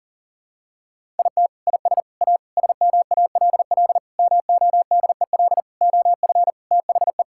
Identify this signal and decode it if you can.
Morse code, heard as a keyed tone, about 37 wpm, 700 Hz. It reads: IT IS A SMALL MODEL OF THE
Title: Sentences from Top 1000 Words - Review Part 5 - 40wpm